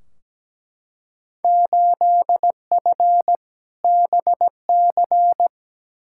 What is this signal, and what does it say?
Morse code, heard as a keyed tone, about 17 wpm, 700 Hz. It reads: 8F BC
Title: Field Day Exchanges Encore - 17wpm